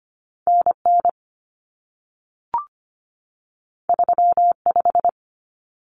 Morse code, heard as a keyed tone, about 25 wpm, 700 Hz. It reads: NN 35
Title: Common Cut Numbers - 2-Digit 25wpm